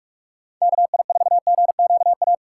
Morse code, heard as a keyed tone, about 54 wpm, 700 Hz. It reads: KI4GQA
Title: Speed Racing - USA 2x3 Call Signs 45wpm